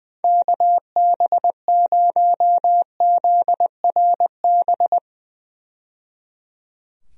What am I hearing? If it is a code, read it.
Morse code, heard as a keyed tone, about 20 wpm, 700 Hz. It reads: KB0ZRB
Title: USA Call Signs - 6 Characters - 20wpm